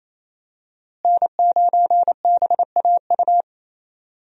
Morse code, heard as a keyed tone, about 28 wpm, 700 Hz. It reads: N9BAU
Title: USA Call Signs - 5 Characters - Repeated 3x - 28wpm